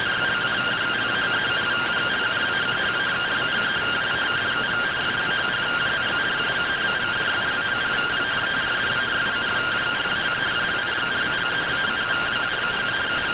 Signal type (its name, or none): two-tone teleprinter signal